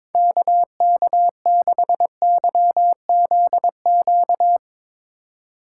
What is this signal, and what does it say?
Morse code, heard as a keyed tone, about 24 wpm, 700 Hz. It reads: KK6YZQ
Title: Speed Racing - USA 2x3 Call Signs 22wpm